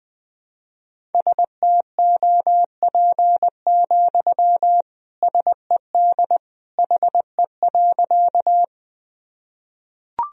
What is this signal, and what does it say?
Morse code, heard as a keyed tone, about 20 wpm, 700 Hz. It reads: STOP, SED HE.
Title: Sentences from Top 300 Words and Abbreviations 20wpm